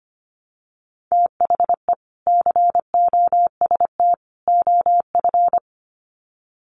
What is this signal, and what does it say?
Morse code, heard as a keyed tone, about 25 wpm, 700 Hz. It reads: THE COST OF
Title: Sets of 3 Words 25wpm